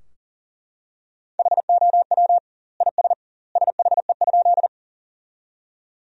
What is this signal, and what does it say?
Morse code, heard as a keyed tone, about 40 wpm, 700 Hz. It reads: HOW IS SHE?
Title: 2x Word-Spacing - Sentences from Top 100 Words - 40wpm